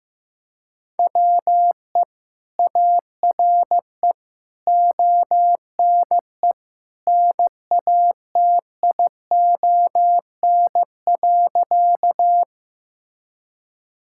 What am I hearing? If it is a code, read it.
Morse code, heard as a keyed tone, about 15 wpm, 700 Hz. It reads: WE ARE ONE NATION.
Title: Sentences from Top 1000 Words - Review Part 6 - 15wpm